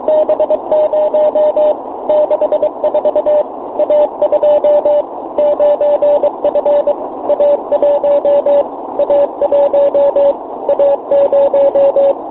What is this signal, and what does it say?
Morse code, heard as a keyed tone, about 23 wpm, 600 Hz, with radio noise. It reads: B0 64 A2 9F A1 A1 A0
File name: fo29bake